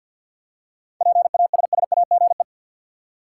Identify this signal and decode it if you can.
Morse code, heard as a keyed tone, about 51 wpm, 700 Hz. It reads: PASSAGE
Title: Seven Letter Words - 50wpm